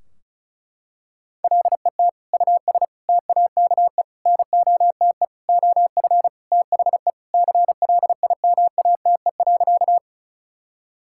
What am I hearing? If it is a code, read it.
Morse code, heard as a keyed tone, about 36 wpm, 700 Hz. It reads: LET US TAKE NOTE OF THE CLIMATE.
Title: Sentences from Top 1600 Words 35wpm